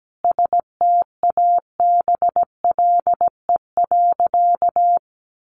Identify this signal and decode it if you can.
Morse code, heard as a keyed tone, about 17 wpm, 700 Hz. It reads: STABLE.
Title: Sentences from Top 1500 Words 17wpm